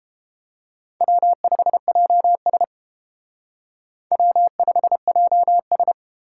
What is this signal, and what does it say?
Morse code, heard as a keyed tone, about 29 wpm, 700 Hz. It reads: W5JS W5JS